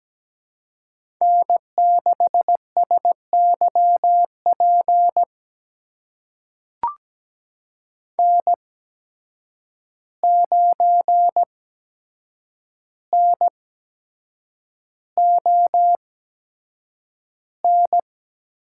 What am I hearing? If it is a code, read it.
Morse code, heard as a keyed tone, about 17 wpm, spaced as for 7 wpm, 700 Hz. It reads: N6SYP N9NON
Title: USA Call Signs - 5 Characters - 8x Character Spacing - 17wpm